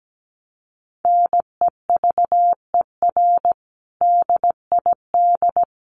Morse code, heard as a keyed tone, about 17 wpm, 700 Hz. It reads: NEVER DID
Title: Sentences from Top 1200 Words 17wpm